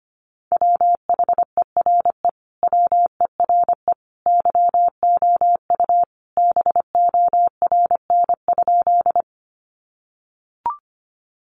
Morse code, heard as a keyed tone, about 25 wpm, 700 Hz. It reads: WHERE WERE YOU BORN?
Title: Sentences from Top 1000 Words 25wpm